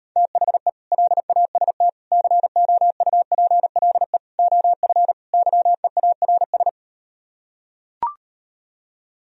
Morse code, heard as a keyed tone, about 38 wpm, 700 Hz. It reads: THE LAST COUPLE OF YEARS